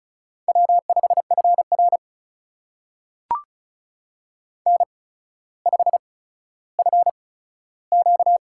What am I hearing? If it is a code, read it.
Morse code, heard as a keyed tone, about 36 wpm, spaced as for 15 wpm, 700 Hz. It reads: W5FR N5FQ
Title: USA Call Signs - 4 Characters - 8x Character Spacing - 35wpm